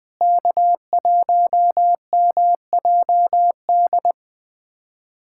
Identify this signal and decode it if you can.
Morse code, heard as a keyed tone, about 21 wpm, 700 Hz. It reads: K1MJD
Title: USA Callsign DE Callsign - 5 Characters - 20wpm